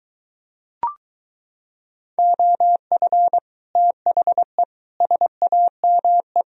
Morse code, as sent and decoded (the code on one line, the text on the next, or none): --- ..-. / - .... / ... .- -- .
OF THE SAME